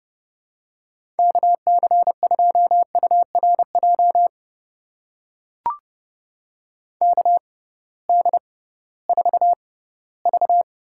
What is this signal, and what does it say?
Morse code, heard as a keyed tone, about 30 wpm, spaced as for 18 wpm, 700 Hz. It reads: KC2URJ KD4V
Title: USA Call Signs - 6 Characters - 6x Character Spacing - 30wpm